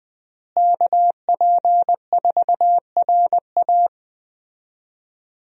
Morse code, heard as a keyed tone, about 20 wpm, 700 Hz. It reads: KP4RA
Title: DXCC-100 Call Signs - 5 Characters - Repeated 3x - 20wpm